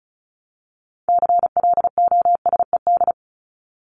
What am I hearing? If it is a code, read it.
Morse code, heard as a keyed tone, about 34 wpm, 700 Hz. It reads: CLOSED
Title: ED Suffix - Top 100 Words with Base Words 35wpm